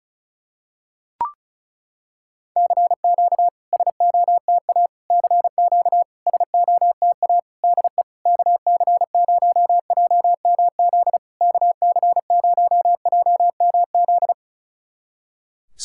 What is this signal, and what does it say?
Morse code, heard as a keyed tone, about 36 wpm, 700 Hz. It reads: CQ SOTA CQ SOTA DE KC0JMZ KC0JMZ
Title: SOTA QSO Practice 35wpm